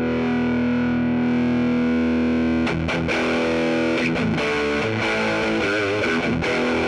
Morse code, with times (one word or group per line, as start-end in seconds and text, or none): none